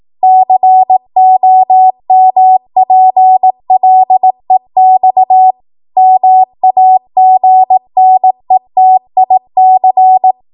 Morse code, heard as a keyed tone, about 18 wpm, 750 Hz. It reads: COMPLEX MAGNETIC
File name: Morse / 18WPM